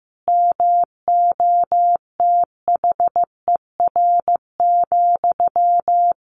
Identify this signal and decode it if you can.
Morse code, heard as a keyed tone, about 15 wpm, 700 Hz. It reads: MOTHER,
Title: Sentences from Top 1500 Words 15wpm